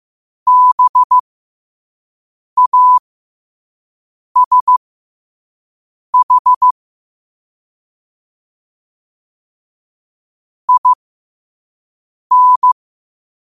Morse code, heard as a keyed tone, about 15 wpm, spaced as for 5 wpm, 1000 Hz. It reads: BASH IN
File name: qotdfiles / 5wpm